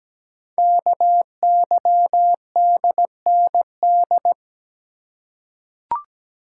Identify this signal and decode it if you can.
Morse code, heard as a keyed tone, about 17 wpm, 700 Hz. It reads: KYDND
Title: Code Groups - 5 Letters - 17wpm